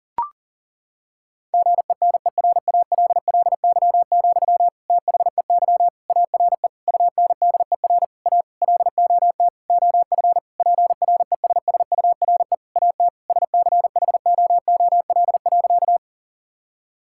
Morse code, heard as a keyed tone, about 37 wpm, 700 Hz. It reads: GENERALLY, THEY ARE UNDER A LOT OF PRESSURE AT SCHOOL.
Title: Sentences from Top 1000 Words - Review Part 6 - 40wpm